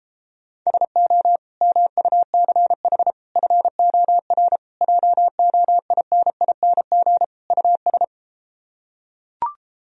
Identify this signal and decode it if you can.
Morse code, heard as a keyed tone, about 33 wpm, 700 Hz. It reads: SO MUCH FOR JOINING US